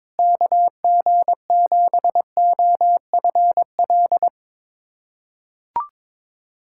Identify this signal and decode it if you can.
Morse code, heard as a keyed tone, about 22 wpm, 700 Hz. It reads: KG7OFL
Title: USA Callsign DE Callsign - 6 Characters - 22wpm